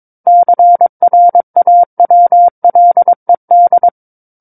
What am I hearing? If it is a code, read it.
Morse code, heard as a keyed tone, about 22 wpm, 700 Hz. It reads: CRAWLED
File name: Book One / 13-Chapter 13 - 22 WPM